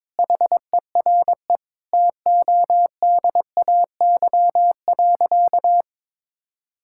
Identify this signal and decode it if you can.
Morse code, heard as a keyed tone, about 22 wpm, 700 Hz. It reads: HERE TODAY.
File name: Sentences with 7-Word Phrases - 22wpm